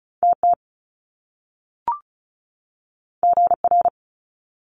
Morse code, heard as a keyed tone, about 36 wpm, 700 Hz. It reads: TT GR
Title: Top Two Letter Combinations in Words 35wpm